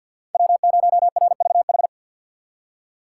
Morse code, heard as a keyed tone, about 51 wpm, 700 Hz. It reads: W0RVH
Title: USA Call Signs - 2x Character Spacing - 50wpm